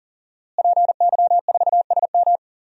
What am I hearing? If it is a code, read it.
Morse code, heard as a keyed tone, about 37 wpm, 700 Hz. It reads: PY4SM
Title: DXCC-100 Call Signs - 5 Characters - Repeated 3x - 40wpm